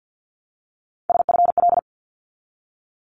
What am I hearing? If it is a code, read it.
Morse code, heard as a keyed tone, about 49 wpm, 700 Hz. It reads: SFL
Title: ARRL Sections 50wpm